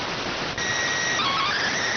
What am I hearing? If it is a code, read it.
SSTV (slow-scan television), PD 290.